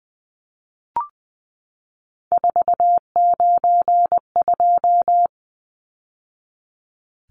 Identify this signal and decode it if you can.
Morse code, heard as a keyed tone, about 20 wpm, 700 Hz. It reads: 492
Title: Numbers - 3-Digit 20wpm